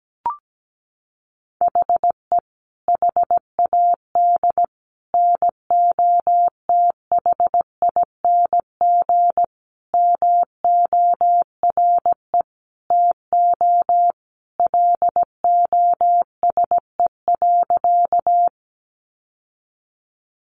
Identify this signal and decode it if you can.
Morse code, heard as a keyed tone, about 17 wpm, 700 Hz. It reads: HE HAD NOTHING MORE TO LOSE.
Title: Sentences from Top 1100 Words 17wpm